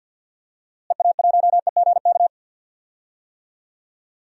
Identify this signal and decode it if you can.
Morse code, heard as a keyed tone, about 52 wpm, 700 Hz. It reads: EA1EGK
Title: DXCC-100 Call Signs - 6 Characters - Repeated 2x - 50wpm